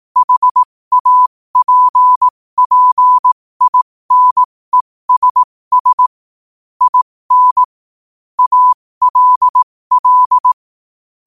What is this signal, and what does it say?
Morse code, heard as a keyed tone, about 18 wpm, spaced as for 15 wpm, 1000 Hz. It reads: HAPPINESS IN ALL
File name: qotdfiles / 15wpm